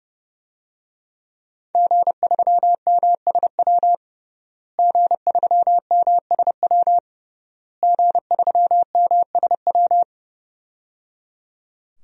Morse code, heard as a keyed tone, about 30 wpm, 700 Hz. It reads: G3MSW G3MSW G3MSW
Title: DXCC-100 Call Signs - 5 Characters - Repeated 3x - 30wpm